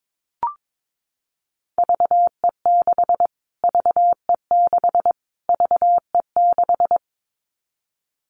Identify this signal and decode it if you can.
Morse code, heard as a keyed tone, about 22 wpm, 700 Hz. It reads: VE6 VE6 VE6
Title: Callsign Prefixes for Top 25 DXCC - Repeated 3x - 22wpm